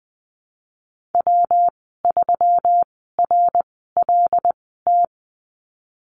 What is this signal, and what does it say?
Morse code, heard as a keyed tone, about 20 wpm, 700 Hz. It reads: W 3 R L T
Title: 154 Lesson 25g - Letter p - Call Signs 20wpm